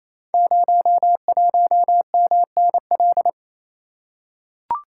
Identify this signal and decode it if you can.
Morse code, heard as a keyed tone, about 27 wpm, 690 Hz. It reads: 01MNL
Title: Code Groups - 5 Letter:Numbers - 28wpm